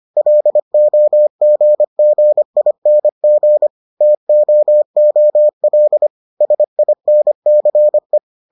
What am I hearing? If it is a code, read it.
Morse code, heard as a keyed tone, about 25 wpm, 600 Hz. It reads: LOGGING TOOL SINCE